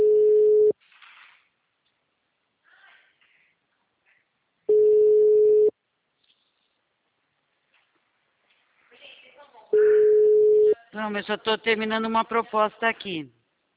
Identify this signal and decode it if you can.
Ringback tone (Europe, 425 Hz), with speech.